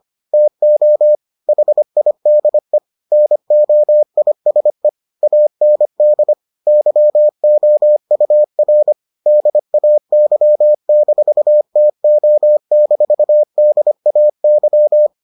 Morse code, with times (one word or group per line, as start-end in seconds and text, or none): 0.3-1.1 TO
1.5-2.8 HIDE
3.1-4.9 NOISE
5.2-6.3 AND
6.7-8.9 YOUR
9.3-15.1 DAY-TO-DAY